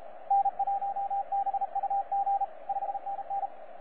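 Morse code, heard as a keyed tone, about 34 wpm, 790 Hz, with radio noise. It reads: N1BUG 5NN
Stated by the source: T47RRC, IOTA NA-086, on 15 m CW, a DX recording by DX World.net.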